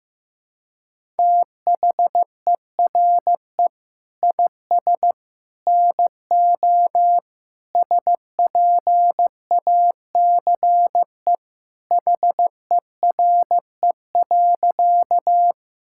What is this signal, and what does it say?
Morse code, heard as a keyed tone, about 15 wpm, 700 Hz. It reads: THERE IS NO SPACE HERE.
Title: Sentences from Top 1000 Words - Review Part 5 - 15wpm